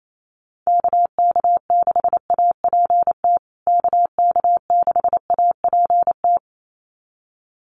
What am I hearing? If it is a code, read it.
Morse code, heard as a keyed tone, about 28 wpm, 700 Hz. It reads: KK6APT KK6APT
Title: USA Call Signs Repeated 2x 28wpm